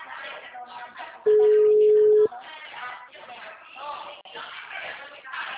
Ringback tone (Europe, 425 Hz).